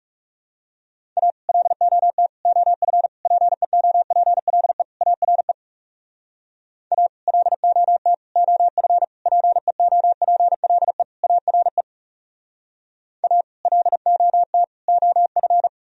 Morse code, heard as a keyed tone, about 41 wpm, 700 Hz. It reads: A LOT OF PEOPLE ARE A LOT OF PEOPLE ARE A LOT OF